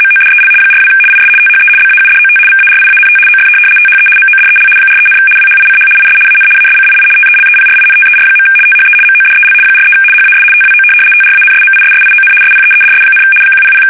RTTY, about 75 baud. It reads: MODEL 42 IS THE BAUDOT VARIANT OF THE MODEL 43, WHICH IS AN ASCII TELEPRINTER. THE MODEL 43 HAS TWO TRANSMISSION SPEEDS, 100
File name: fsk850-75B / teletype-corp